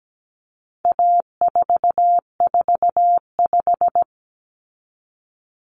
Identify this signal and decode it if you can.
Morse code, heard as a keyed tone, about 17 wpm, 700 Hz. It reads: A445
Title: Common Cut Numbers - 4-Digit 17wpm